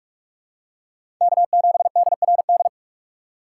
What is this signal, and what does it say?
Morse code, heard as a keyed tone, about 42 wpm, 700 Hz. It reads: K7DRD
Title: USA Call Signs - 4x Character Spacing - 45wpm